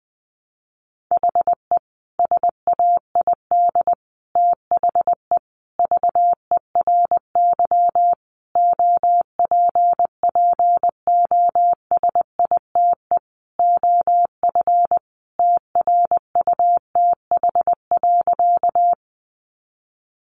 Morse code, heard as a keyed tone, about 20 wpm, 700 Hz. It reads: HE SAID THE VERY OPPOSITE OF TRUTH.
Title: Sentences from Top 1300 Words 20wpm